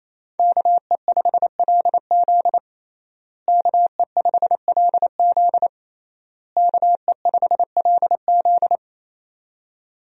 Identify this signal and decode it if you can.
Morse code, heard as a keyed tone, about 28 wpm, 700 Hz. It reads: KE5LZ KE5LZ KE5LZ
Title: USA Call Signs - 5 Characters - Repeated 3x - 28wpm